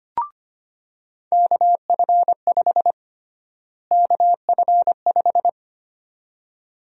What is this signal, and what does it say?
Morse code, heard as a keyed tone, about 25 wpm, 700 Hz. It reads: KF5 KF5
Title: USA Call Sign Prefixes - Repeated 2x - 25wpm